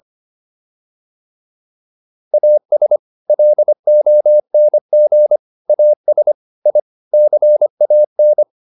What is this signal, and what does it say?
Morse code, heard as a keyed tone, about 25 wpm, 600 Hz. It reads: AS LONG AS I CAN